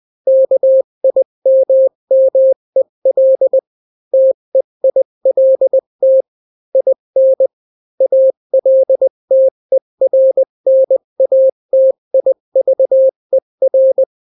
Morse code, heard as a keyed tone, about 20 wpm, 530 Hz. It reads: KIMMEL TEILT IN ALTERNATIVER
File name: daily / 18wpm